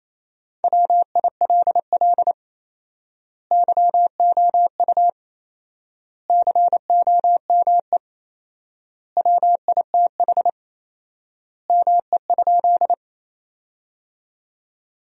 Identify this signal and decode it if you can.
Morse code, heard as a keyed tone, about 28 wpm, 700 Hz. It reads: WILL YOU COME WITH ME?
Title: 4x Word-Spacing - Sentences from Top 200 Words - 28wpm